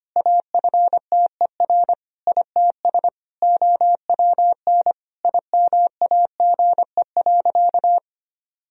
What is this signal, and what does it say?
Morse code, heard as a keyed tone, about 25 wpm, 700 Hz. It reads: AFTER ITS OWN IMAGE.
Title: Sentences from Top 1000 Words - Review Part 4 - 25wpm